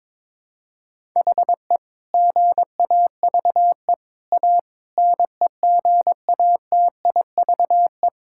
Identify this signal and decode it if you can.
Morse code, heard as a keyed tone, about 21 wpm, 700 Hz. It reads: HE GAVE A NEGATIVE
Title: Sentences from Top 1000 Words - Review Part 6 - 22wpm